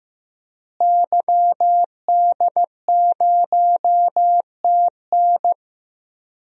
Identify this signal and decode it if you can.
Morse code, heard as a keyed tone, about 15 wpm, 700 Hz. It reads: YD0TN
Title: DXCC-100 Call Signs - 5 Characters - Repeated 3x - 15wpm